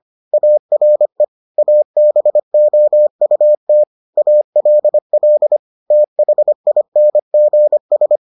Morse code, heard as a keyed tone, about 25 wpm, 600 Hz. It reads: ARE ABOUT ALL THINGS